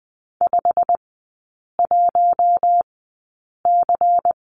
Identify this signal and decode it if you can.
Morse code, heard as a keyed tone, about 20 wpm, spaced as for 8 wpm, 700 Hz. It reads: 51C
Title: 100 Lesson 17e - Letter u - Three Characters 20wpm